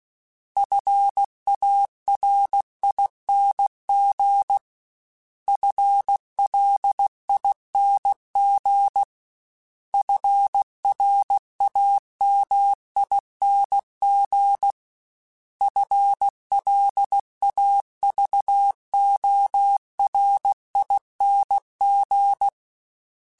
Morse code, heard as a keyed tone, about 16 wpm, 800 Hz. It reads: FARING FLING FRAMING FLAVORING
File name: Special INGs / Special INGs-mp3s / Sngl ing @15